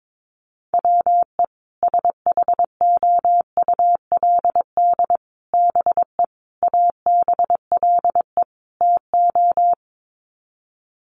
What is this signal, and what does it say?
Morse code, heard as a keyed tone, about 22 wpm, 700 Hz. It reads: WE SHOULD BE ABLE TO
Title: Sets of 5 Words Encore2 22wpm